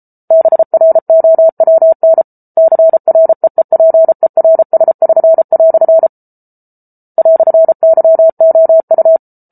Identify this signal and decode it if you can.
Morse code, heard as a keyed tone, about 34 wpm, 650 Hz. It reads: BROWN CREEPERS<SN>" "YOU